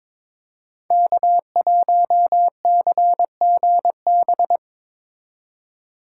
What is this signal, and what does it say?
Morse code, heard as a keyed tone, about 22 wpm, 700 Hz. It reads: K1CGB
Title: Code Groups - 5 Letter:Numbers - 22wpm